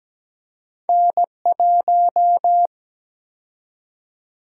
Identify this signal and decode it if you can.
Morse code, heard as a keyed tone, about 17 wpm, 700 Hz. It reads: N1
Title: USA Call Sign Prefixes - 17wpm